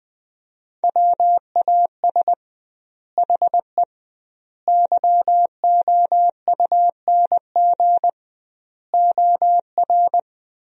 Morse code, heard as a keyed tone, about 20 wpm, 700 Hz. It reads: WAS HE YOUNG OR
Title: 2x Word-Spacing - Sentences from Top 400 Words - 20wpm